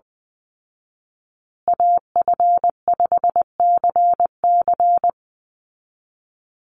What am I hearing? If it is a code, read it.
Morse code, heard as a keyed tone, about 20 wpm, 700 Hz. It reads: AF5CC
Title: Call Signs Repeated 3x - 20wpm